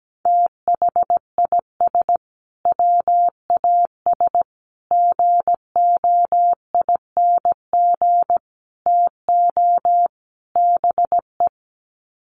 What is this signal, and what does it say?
Morse code, heard as a keyed tone, about 17 wpm, 700 Hz. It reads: THIS WAS GOING TO BE
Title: Sets of 5 Words Encore2 17wpm